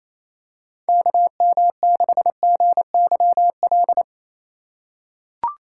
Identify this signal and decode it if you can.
Morse code, heard as a keyed tone, about 28 wpm, 700 Hz. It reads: KM6GYL
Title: USA Call Signs - 3x Character Spacing - 28wpm